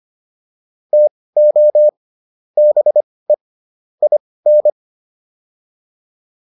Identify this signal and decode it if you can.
Morse code, heard as a keyed tone, about 25 wpm, spaced as for 18 wpm, 600 Hz. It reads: TO BE IN